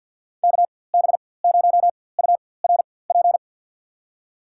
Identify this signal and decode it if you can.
Morse code, heard as a keyed tone, about 49 wpm, spaced as for 23 wpm, 700 Hz. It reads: KB0URP
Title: USA Call Signs - 6 Characters - 4x Character Spacing - 50wpm